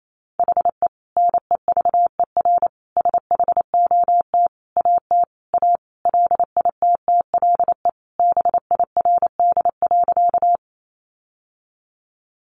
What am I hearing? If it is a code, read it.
Morse code, heard as a keyed tone, about 28 wpm, 700 Hz. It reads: HE NEVER SHOT AT A LITTLE BIRD.
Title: Sentences from Top 1100 Words 28wpm